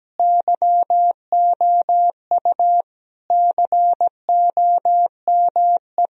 Morse code, heard as a keyed tone, about 17 wpm, 700 Hz. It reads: YOU COME
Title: Sentences from Top 1000 Words - Review Part 1 - 17wpm